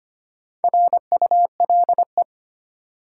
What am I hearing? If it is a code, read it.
Morse code, heard as a keyed tone, about 25 wpm, 700 Hz. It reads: RULE
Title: Sentences from Top 400 Words - New Word Drill - 25wpm